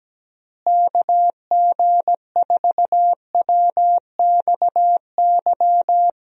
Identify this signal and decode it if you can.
Morse code, heard as a keyed tone, about 17 wpm, 700 Hz. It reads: KG4WXY
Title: USA Call Signs - 6 Characters - Repeated 3x - 17wpm